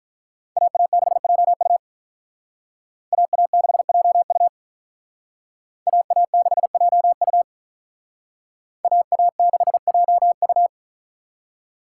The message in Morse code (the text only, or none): AA6JU AA6JU AA6JU AA6JU